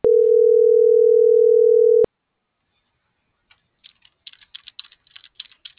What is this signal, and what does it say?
Ringback tone (North America), 440 + 480 Hz.